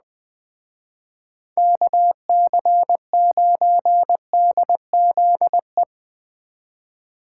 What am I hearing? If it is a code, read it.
Morse code, heard as a keyed tone, about 20 wpm, 700 Hz. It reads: KC9DZE
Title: Speed Racing - USA Call Signs 17wpm